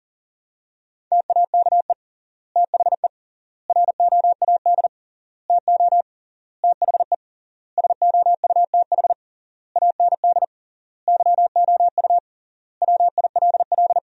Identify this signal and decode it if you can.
Morse code, heard as a keyed tone, about 40 wpm, 700 Hz. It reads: TAKE THE ROAD TO THE SOUTH AND YOU WILL
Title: 3x Word-Spacing - Sentences from Top 400 Words - 40wpm